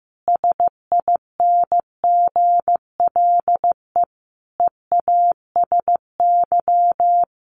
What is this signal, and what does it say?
Morse code, heard as a keyed tone, about 15 wpm, 700 Hz. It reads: SINGLE EASY
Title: Sentences from Top 1300 Words 15wpm